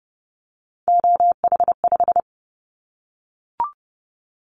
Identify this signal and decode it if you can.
Morse code, heard as a keyed tone, about 30 wpm, 700 Hz. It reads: OH5
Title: Callsign Prefixes for Top 25 DXCC - 30wpm